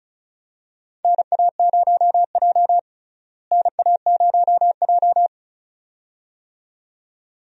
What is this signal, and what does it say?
Morse code, heard as a keyed tone, about 36 wpm, 700 Hz. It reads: NA0J NA0J
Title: USA Call Signs - 4 Characters - Repeated 2x - 35wpm